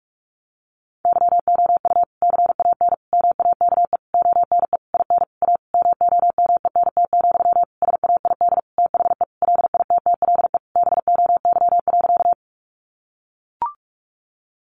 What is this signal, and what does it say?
Morse code, heard as a keyed tone, about 44 wpm, 700 Hz. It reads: YOU CAN MAKE ONE IN A MOMENT, SAID THE LITTLE BOY.
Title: Sentences from Top 1000 Words 45wpm